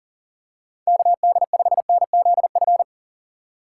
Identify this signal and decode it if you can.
Morse code, heard as a keyed tone, about 38 wpm, 700 Hz. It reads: KD5NZF
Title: USA Call Signs - 2x Character Spacing - 40wpm